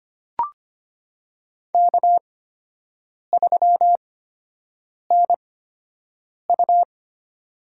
Morse code, heard as a keyed tone, about 25 wpm, spaced as for 7 wpm, 700 Hz. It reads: K3NU
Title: USA Call Signs - 8x Character Spacing - 25wpm